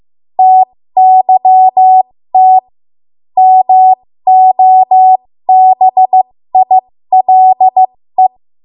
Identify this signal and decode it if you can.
Morse code, heard as a keyed tone, about 15 wpm, spaced as for 13 wpm, 750 Hz. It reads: TYT MOBILE
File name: Morse / 13WPM